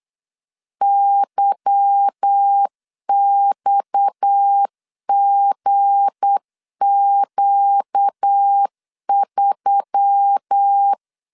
Morse code, heard as a keyed tone, about 8 wpm, 800 Hz. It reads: YXGQ3
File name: LEZIONI / 8 YJ45 / 8 Y 8wpm 3245